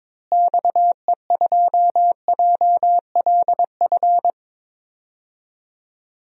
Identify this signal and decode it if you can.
Morse code, heard as a keyed tone, about 22 wpm, 700 Hz. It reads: XE2JLF
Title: DXCC-100 Call Signs - 6 Characters - Repeated 3x - 22wpm